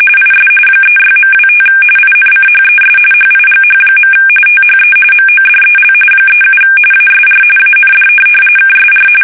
RTTY, about 45.45 baud. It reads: IS 36.25 INCHES HIGH. THE MODEL 37 ASR AND KSR
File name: fsk850 / teletype-corp